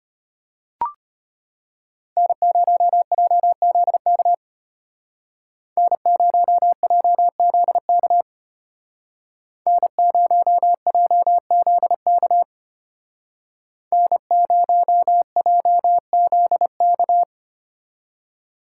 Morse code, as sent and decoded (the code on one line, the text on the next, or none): -. ----- .--- --.. -.- / -. ----- .--- --.. -.- / -. ----- .--- --.. -.- / -. ----- .--- --.. -.-
N0JZK N0JZK N0JZK N0JZK